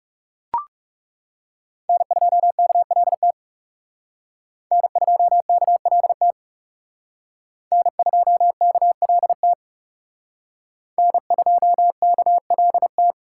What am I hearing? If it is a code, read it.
Morse code, heard as a keyed tone, about 37 wpm, 700 Hz. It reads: N2KLT N2KLT N2KLT N2KLT
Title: Speed Racing - USA Call Signs 30wpm